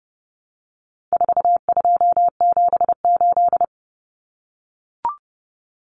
Morse code, heard as a keyed tone, about 30 wpm, 700 Hz. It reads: 4278
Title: Numbers - 4-Digit 30wpm